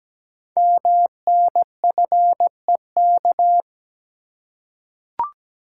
Morse code, heard as a keyed tone, about 17 wpm, 700 Hz. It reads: MNFEK
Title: Code Groups - 5 Letters - 17wpm